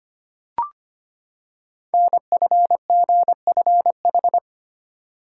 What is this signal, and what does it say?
Morse code, heard as a keyed tone, about 25 wpm, 700 Hz. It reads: NFGFH
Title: Code Groups - 5 Letters - 25wpm